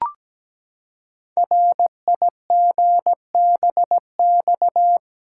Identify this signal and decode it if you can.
Morse code, heard as a keyed tone, about 17 wpm, 700 Hz. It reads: RIGBX
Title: Code Groups - 5 Letters - 17wpm